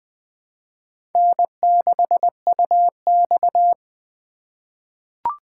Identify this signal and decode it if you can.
Morse code, heard as a keyed tone, about 20 wpm, 700 Hz. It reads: N6UX
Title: USA Call Signs - 4 Characters - 2x Character Spacing - 20wpm